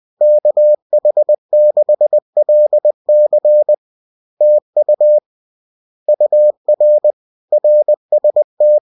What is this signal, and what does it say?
Morse code, heard as a keyed tone, about 20 wpm, 600 Hz. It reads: KH6LC TU UR RST